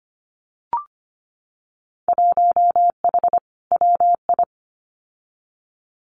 Morse code, heard as a keyed tone, about 25 wpm, 700 Hz. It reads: 1H WI
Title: Winter Field Day Exchanges 25wpm